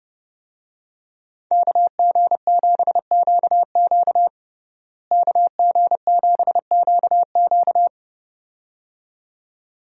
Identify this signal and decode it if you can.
Morse code, heard as a keyed tone, about 30 wpm, 700 Hz. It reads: KG7QQ KG7QQ
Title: USA Call Signs - 5 Characters - Repeated 2x - 30wpm